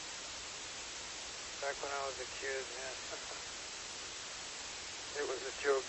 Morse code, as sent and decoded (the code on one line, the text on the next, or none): none